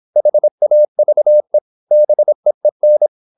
Morse code, heard as a keyed tone, about 26 wpm, 600 Hz. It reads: HAVE BEEN
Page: Never 26wpm | CW med Gnister